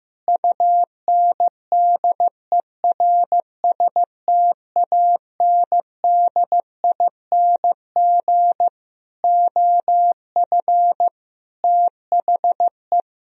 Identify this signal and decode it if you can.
Morse code, heard as a keyed tone, about 14 wpm, 700 Hz. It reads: UNDERSTANDING OF THE
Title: Sentences from Top 1000 Words - Review Part 3 - 15wpm